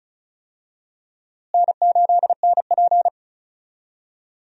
Morse code, heard as a keyed tone, about 34 wpm, 700 Hz. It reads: N8NP
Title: USA Call Signs - 4 Characters - Repeated 2x - 35wpm